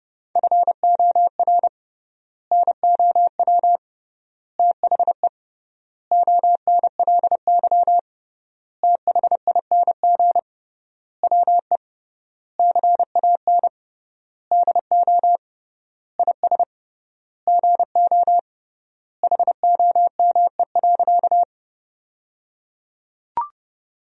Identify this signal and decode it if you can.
Morse code, heard as a keyed tone, about 30 wpm, 700 Hz. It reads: FOR NOW THE ONLY THING WE CAN DO IS GO HOME.
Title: 3x Word-Spacing - Sentences from Top 200 Words - 30wpm